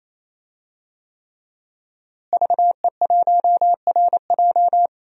Morse code, heard as a keyed tone, about 27 wpm, 700 Hz. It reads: VE1RJ
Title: DXCC-100 Call Signs - 5 Characters - Repeated 3x - 28wpm